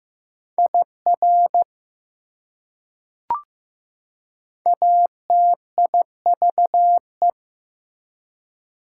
Morse code, heard as a keyed tone, about 15 wpm, 700 Hz. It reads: IR ATIVE
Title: Prefixes and Suffixes 15wpm